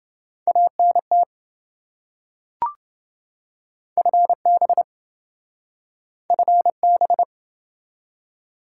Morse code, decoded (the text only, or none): ANT FB FB